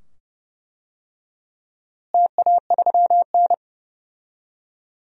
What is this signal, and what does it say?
Morse code, heard as a keyed tone, about 30 wpm, 700 Hz. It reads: TA3N
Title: DXCC-100 Call Signs - 4 Characters - Repeated 2x - 30wpm